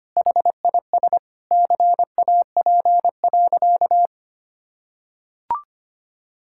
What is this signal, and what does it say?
Morse code, heard as a keyed tone, about 25 wpm, 700 Hz. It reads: HIS CAP.
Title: Sentences from Top 1600 Words 25wpm